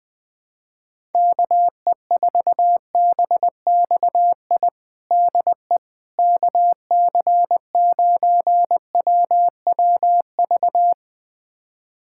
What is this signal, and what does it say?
Morse code, heard as a keyed tone, about 21 wpm, 700 Hz. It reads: KE4BXI DE KC9WWV
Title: USA Callsign DE Callsign 20wpm